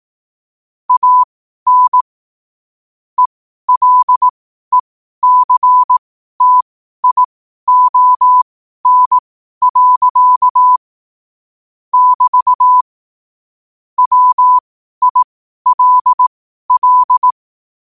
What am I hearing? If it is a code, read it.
Morse code, heard as a keyed tone, about 18 wpm, spaced as for 12 wpm, 1000 Hz. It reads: AN ELECTION. <BT> WILL
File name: qotdfiles / 12wpm